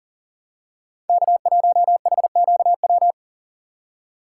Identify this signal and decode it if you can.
Morse code, heard as a keyed tone, about 40 wpm, 700 Hz. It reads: K1HQW
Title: USA Call Signs - 5 Characters - 3x Character Spacing - 40wpm